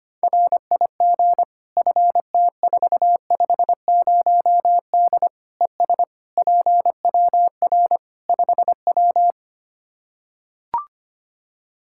Morse code, heard as a keyed tone, about 25 wpm, 700 Hz. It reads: RIG FT450D ES PWR 5W
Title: Practice QSO - Exchange 2 - Info - 25wpm